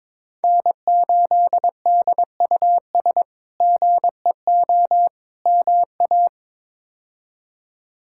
Morse code, heard as a keyed tone, about 22 wpm, 700 Hz. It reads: N8DUS GEO MA
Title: K1USN SST QSO Practice - 22wpm